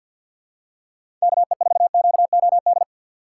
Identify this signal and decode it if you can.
Morse code, heard as a keyed tone, about 47 wpm, 700 Hz. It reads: KE4QOD
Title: USA Call Signs - 6 Characters - 4x Character Spacing - 50wpm